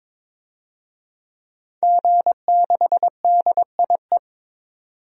Morse code, heard as a keyed tone, about 22 wpm, 700 Hz. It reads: G6DIE
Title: DXCC-100 Call Signs - 5 Characters - 22wpm